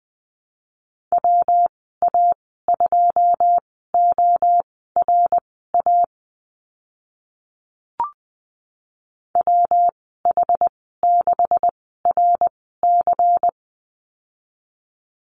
Morse code, heard as a keyed tone, about 20 wpm, spaced as for 15 wpm, 700 Hz. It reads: WA2ORA WH6RC
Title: 167 Lesson 27g - Number 7 - Call Signs 20wpm